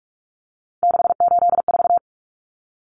Morse code, heard as a keyed tone, about 44 wpm, 700 Hz. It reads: 684
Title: Numbers - 3-Digit 45wpm